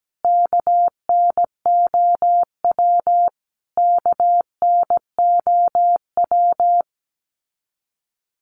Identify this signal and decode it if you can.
Morse code, heard as a keyed tone, about 17 wpm, 700 Hz. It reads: KNOW KNOW
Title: Top 200 Words Repeated 17wpm